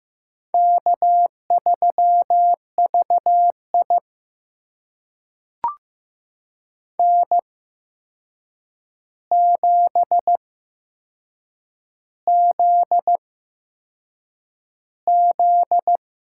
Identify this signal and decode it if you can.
Morse code, heard as a keyed tone, about 15 wpm, spaced as for 6 wpm, 700 Hz. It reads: K3VI N7ZZ
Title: USA Call Signs - 4 Characters - 8x Character Spacing - 15wpm